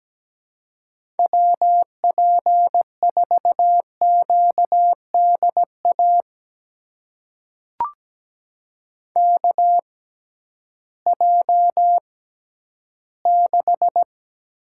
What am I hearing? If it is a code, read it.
Morse code, heard as a keyed tone, about 17 wpm, spaced as for 11 wpm, 700 Hz. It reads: WP4QDA KJ6
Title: USA Call Signs - 6 Characters - 6x Character Spacing - 17wpm